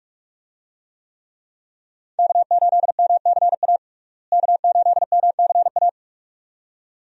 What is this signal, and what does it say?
Morse code, heard as a keyed tone, about 42 wpm, 700 Hz. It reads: K8MCA K8MCA
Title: USA Call Signs - 5 Characters - Repeated 2x - 45wpm